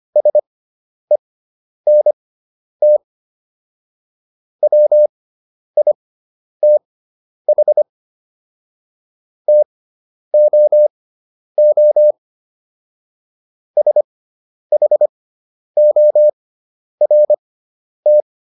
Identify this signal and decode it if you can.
Morse code, heard as a keyed tone, about 25 wpm, spaced as for 10 wpm, 600 Hz. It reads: SENT WITH TOO SHORT